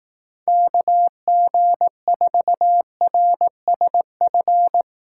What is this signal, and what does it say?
Morse code, heard as a keyed tone, about 19 wpm, 700 Hz. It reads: KG4RSF
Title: Speed Racing - USA 2x3 Call Signs 15wpm